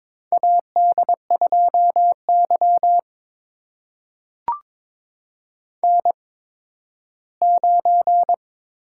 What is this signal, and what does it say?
Morse code, heard as a keyed tone, about 23 wpm, spaced as for 13 wpm, 700 Hz. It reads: AD2Y N9
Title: USA Call Signs - 8x Character Spacing - 22wpm